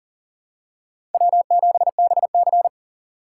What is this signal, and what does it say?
Morse code, heard as a keyed tone, about 44 wpm, 700 Hz. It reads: W7BC
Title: USA Call Signs - 4 Characters - 4x Character Spacing - 40wpm